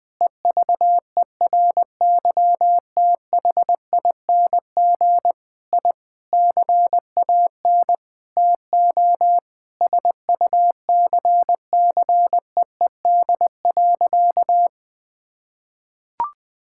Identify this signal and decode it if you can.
Morse code, heard as a keyed tone, about 20 wpm, 700 Hz. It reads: EVERYTHING I CAN TO SUCCEED.
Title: Sentences with 8-Word Phrases - Spoken Beginning - 20wpm